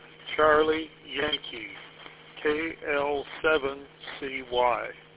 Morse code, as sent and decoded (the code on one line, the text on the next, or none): none